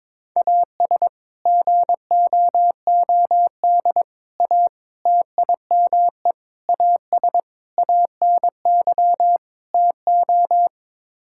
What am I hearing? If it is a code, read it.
Morse code, heard as a keyed tone, about 21 wpm, 700 Hz. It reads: AS GOOD A TIME AS ANY TO
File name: practice / 22wpm / Sets of 7 Words - 22wpm